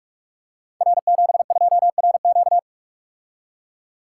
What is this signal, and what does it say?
Morse code, heard as a keyed tone, about 44 wpm, 700 Hz. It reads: R72RQ
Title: Code Groups - 5 Letter:Numbers - 45wpm